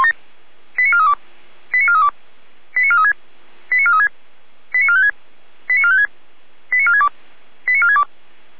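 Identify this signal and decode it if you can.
Five-tone selective call (ZVEI), heard as quick runs of five tones, tones of about 70 ms. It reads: 89432 89432 89436 89436 89456 89456 89462 89462
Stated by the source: ZVEI five-tone selective call received via SDR by Feuerwehr Orlamünde.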